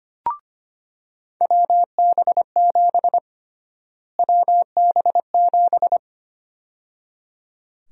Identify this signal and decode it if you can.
Morse code, heard as a keyed tone, about 25 wpm, 700 Hz. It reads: WB7 WB7
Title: USA Call Sign Prefixes - Repeated 2x - 25wpm